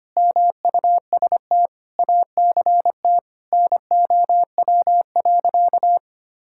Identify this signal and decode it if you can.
Morse code, heard as a keyed tone, about 25 wpm, 700 Hz. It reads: MUST ACT NOW.
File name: practice / 25wpm / Sentences with 9-Word Phrases - 25wpm